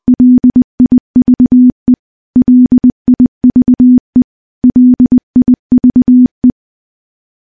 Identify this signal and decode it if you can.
Morse code, heard as a keyed tone, about 20 wpm, 250 Hz. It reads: LIVE LIVE LIVE
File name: morse / morse-training-20WPM